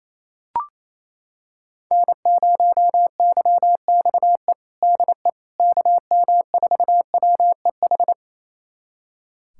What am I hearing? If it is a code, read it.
Morse code, heard as a keyed tone, about 27 wpm, 700 Hz. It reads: N0YXE DE KM4WEH
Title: USA Callsign DE Callsign 28wpm